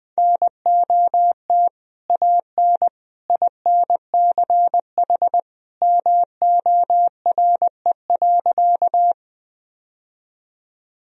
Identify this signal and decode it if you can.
Morse code, heard as a keyed tone, about 20 wpm, 700 Hz. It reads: NOT AN INCH MORE.
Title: Sentences from Top 1000 Words - Review Part 3 - 20wpm